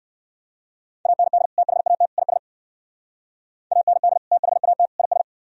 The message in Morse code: AND THAT IS AND THAT IS

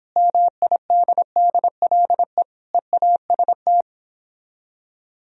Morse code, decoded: MIDDLE EAST

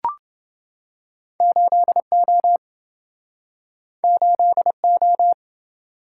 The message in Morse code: 8O 8O

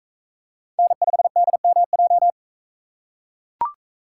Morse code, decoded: N5DMJ

